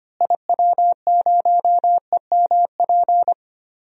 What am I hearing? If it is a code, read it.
Morse code, heard as a keyed tone, about 25 wpm, 700 Hz. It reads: IW0EMP
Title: DXCC-100 Call Signs - 6 Characters - Repeated 3x - 25wpm